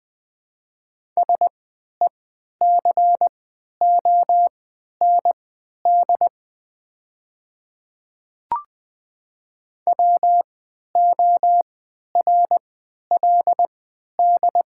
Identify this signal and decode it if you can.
Morse code, heard as a keyed tone, about 20 wpm, spaced as for 11 wpm, 700 Hz. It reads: SECOND WORLD
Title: 3x Character Spacing - Top 300 Words - 20wpm